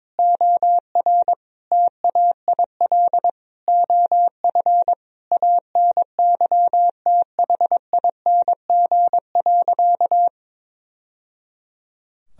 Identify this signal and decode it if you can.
Morse code, heard as a keyed tone, about 22 wpm, 700 Hz. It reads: OR TAIL OF ANYTHING.
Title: Sentences from Top 1000 Words - Review Part 3 - 22wpm